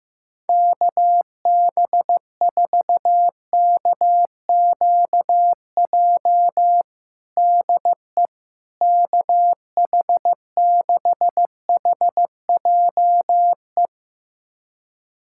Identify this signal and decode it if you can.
Morse code, heard as a keyed tone, about 15 wpm, 700 Hz. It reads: KB4KQJ DE KH6HJE
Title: USA Callsign DE Callsign 15wpm